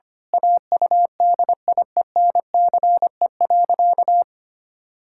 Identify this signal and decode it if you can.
Morse code, heard as a keyed tone, about 25 wpm, 700 Hz. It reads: AUDIENCE.
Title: Pangram Challenge v2 - 25wpm